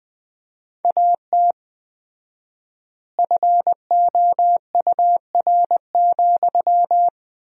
Morse code, heard as a keyed tone, about 21 wpm, 700 Hz. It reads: AT FOUR,
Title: 4x Word-Spacing - Sentences from Top 300 Words - 20wpm